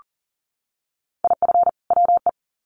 Morse code, heard as a keyed tone, about 40 wpm, 700 Hz. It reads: IF WE
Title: Sets of 2 Words 40wpm